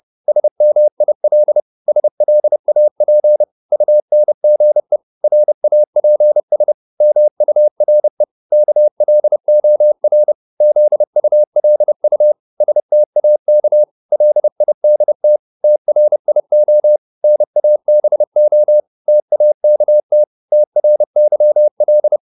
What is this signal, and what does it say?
Morse code, heard as a keyed tone, about 30 wpm, 600 Hz. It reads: SMIL SLAP UNGE RAPS MURE KLOR ZULU STAK LIDT TRIO NABO TAKT TRYL
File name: Korte ord DK 30wpm